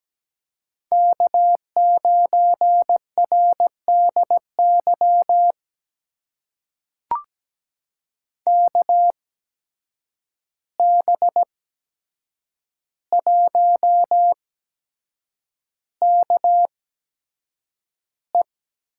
Morse code, heard as a keyed tone, about 17 wpm, spaced as for 7 wpm, 700 Hz. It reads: K9RDY KB1KE
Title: USA Call Signs - 5 Characters - 8x Character Spacing - 17wpm